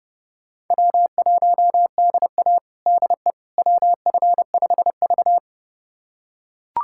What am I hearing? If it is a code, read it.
Morse code, heard as a keyed tone, about 31 wpm, 700 Hz. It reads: W1DA DE WF5V
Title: USA Callsign DE Callsign - 4 Characters - 30wpm